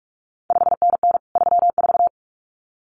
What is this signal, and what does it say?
Morse code, heard as a keyed tone, about 46 wpm, 700 Hz. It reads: 5NN 34
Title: CQ WW DX Contest Exchange Encore - 45wpm